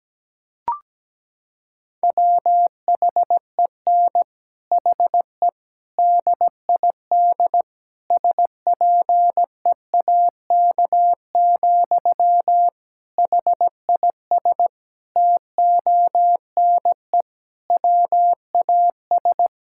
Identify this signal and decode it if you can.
Morse code, heard as a keyed tone, about 17 wpm, 700 Hz. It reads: WHEN HE DID SPEAK, HIS TONE WAS
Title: Sentences from Top 1600 Words 17wpm